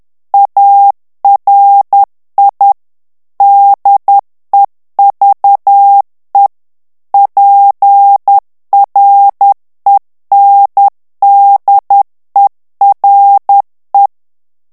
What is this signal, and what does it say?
Morse code, heard as a keyed tone, about 10 wpm, 800 Hz. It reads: ARI DEVE PRENDERE
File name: ITALIEN texte clair 2 vitesse 10 mots minute